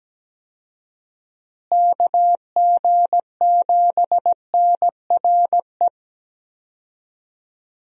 Morse code, heard as a keyed tone, about 17 wpm, 700 Hz. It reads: KG7NRE
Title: USA Call Signs - 6 Characters - Repeated 3x - 17wpm